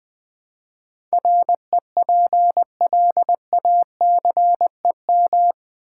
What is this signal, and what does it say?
Morse code, heard as a keyed tone, about 21 wpm, 700 Hz. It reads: REPLACEM
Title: Build A Word - 4 Letters At A Time - 11-Letter Words - 20wpm